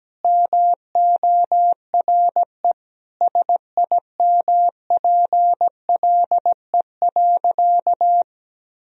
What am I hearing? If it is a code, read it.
Morse code, heard as a keyed tone, about 17 wpm, 700 Hz. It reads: MORE SIMPLE.
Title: Sentences from Top 400 Words and Abbreviations 17wpm